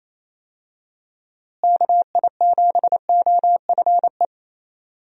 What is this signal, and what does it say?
Morse code, heard as a keyed tone, about 28 wpm, 700 Hz. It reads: KI7OFE
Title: USA Call Signs - 6 Characters - Repeated 3x - 28wpm